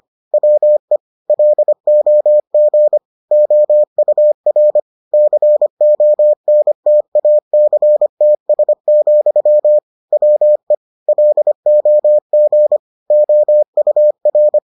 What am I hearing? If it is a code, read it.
Morse code, heard as a keyed tone, about 25 wpm, 600 Hz. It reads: WE LOG OUR CONTACTS, WE LOG OUR